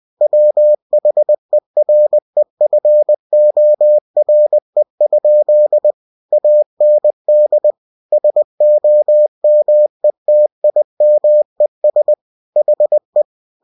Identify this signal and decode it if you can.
Morse code, heard as a keyed tone, about 20 wpm, 600 Hz. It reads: WHEREFORE? AND SOMETIMES HE
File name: cw-books / winnie-the-pooh / winnie-20wpm-page-0026